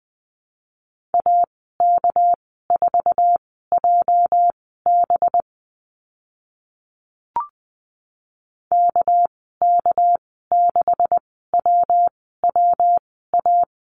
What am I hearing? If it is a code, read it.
Morse code, heard as a keyed tone, about 20 wpm, spaced as for 14 wpm, 700 Hz. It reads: AK4JB KK6WWA
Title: 231 Lesson 37g - Letter x - Call Signs 20wpm